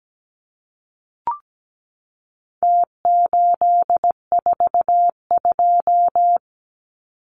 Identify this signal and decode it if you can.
Morse code, heard as a keyed tone, about 17 wpm, 700 Hz. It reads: T842
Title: Numbers - Contest Style - 4-Digit 17wpm